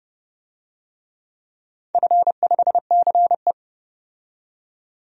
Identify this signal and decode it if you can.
Morse code, heard as a keyed tone, about 32 wpm, 700 Hz. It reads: F5CE
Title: DXCC-100 Call Signs - 4 Characters - 30wpm